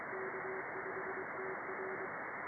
Morse code, heard as a keyed tone, about 23 wpm, 380 Hz, with radio noise. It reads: K5TR
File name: contests / sept-vhf-k5tr- -2200